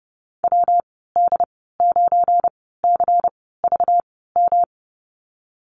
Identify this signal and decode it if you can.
Morse code, heard as a keyed tone, about 30 wpm, spaced as for 17 wpm, 700 Hz. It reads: WD9CVM
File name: Lesson 37g - Letter x - Call Signs 30wpm